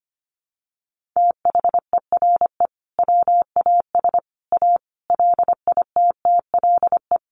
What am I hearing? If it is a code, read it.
Morse code, heard as a keyed tone, about 25 wpm, 700 Hz. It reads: THERE WAS A LITTLE
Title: Sentences from Top 1500 Words 25wpm